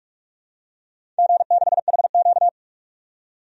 Morse code, heard as a keyed tone, about 47 wpm, 700 Hz. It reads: G6HQ
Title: DXCC-100 Call Signs - 4 Characters - 45wpm